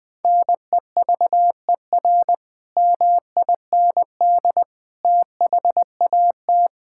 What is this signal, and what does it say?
Morse code, heard as a keyed tone, about 19 wpm, 700 Hz. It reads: NEVER MIND THAT
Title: Sentences from Top 1000 Words - Review Part 5 - 20wpm